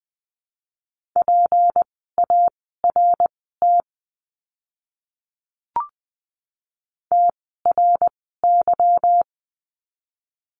Morse code, heard as a keyed tone, about 20 wpm, spaced as for 15 wpm, 700 Hz. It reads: PART TRY
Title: 192 Lesson 31f - Letter v - Words 20wpm